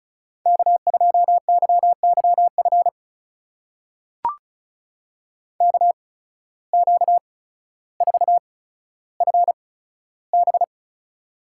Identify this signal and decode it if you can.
Morse code, heard as a keyed tone, about 36 wpm, spaced as for 15 wpm, 700 Hz. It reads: K2YYF KQ4FB
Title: USA Call Signs - 5 Characters - 8x Character Spacing - 35wpm